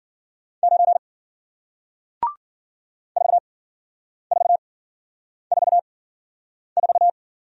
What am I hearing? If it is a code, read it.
Morse code, heard as a keyed tone, about 55 wpm, spaced as for 9 wpm, 700 Hz. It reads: 9 4444